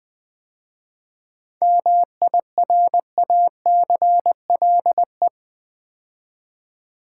Morse code, heard as a keyed tone, about 20 wpm, 700 Hz. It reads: MIRACLE
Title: 7 Letter Words - First 3 Letters Given - 20wpm